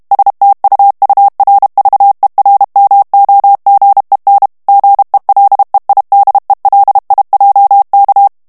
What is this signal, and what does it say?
Morse code, heard as a keyed tone, about 32 wpm, 800 Hz. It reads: STUURVERMOGEN GELEIDELIJK
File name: HOLLANDAIS texte clair 2 vitesse 30 mots minute